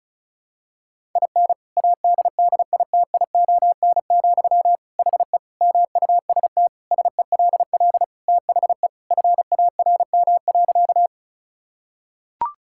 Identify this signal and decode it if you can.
Morse code, heard as a keyed tone, about 36 wpm, 700 Hz. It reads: IN ADDITION, HE MUST SELL THE FARM.
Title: Sentences from Top 1000 Words - Review Part 6 - 35wpm